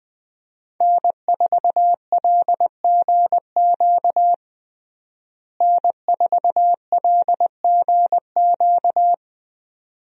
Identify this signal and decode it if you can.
Morse code, heard as a keyed tone, about 20 wpm, 700 Hz. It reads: N4LGQ N4LGQ
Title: USA Call Signs - 5 Characters - Repeated 2x - 20wpm